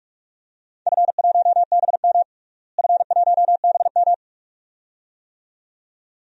Morse code, heard as a keyed tone, about 42 wpm, 700 Hz. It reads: F1BM F1BM
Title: DXCC-100 Call Signs - 4 Characters - Repeated 2x - 45wpm